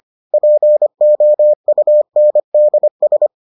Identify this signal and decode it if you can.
Morse code, heard as a keyed tone, about 25 wpm, 600 Hz. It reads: POUNDS